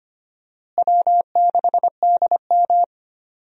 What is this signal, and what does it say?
Morse code, heard as a keyed tone, about 25 wpm, 700 Hz. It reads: W6DM
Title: USA Call Signs - 4 Characters - Repeated 3x - 25wpm